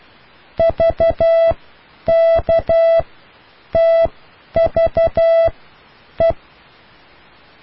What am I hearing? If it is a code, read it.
Morse code, heard as a keyed tone, about 12 wpm, spaced as for 8 wpm, 660 Hz. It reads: VKTVE